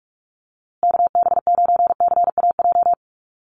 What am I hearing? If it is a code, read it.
Morse code, heard as a keyed tone, about 44 wpm, 700 Hz. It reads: KB9CAJ
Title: USA Call Signs Repeated 2x 45wpm